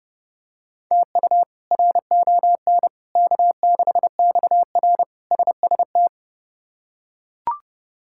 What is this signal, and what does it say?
Morse code, heard as a keyed tone, about 30 wpm, 700 Hz. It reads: TU RON K6XR SST
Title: K1USN SST QSO Practice - 30wpm